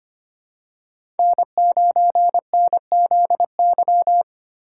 Morse code, heard as a keyed tone, about 25 wpm, 700 Hz. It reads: N9NZY
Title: USA Call Signs - 5 Characters - 3x Character Spacing - 25wpm